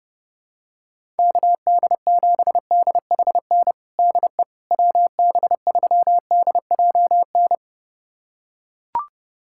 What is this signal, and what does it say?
Morse code, heard as a keyed tone, about 30 wpm, 700 Hz. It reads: KD7DHN DE WB3DJN
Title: USA Callsign DE Callsign 30wpm